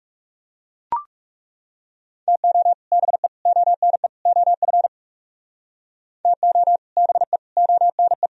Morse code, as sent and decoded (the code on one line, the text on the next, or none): - --- / -... / --- -. / --- ..-. / - --- / -... / --- -. .
TO BE ONE OF TO BE ONE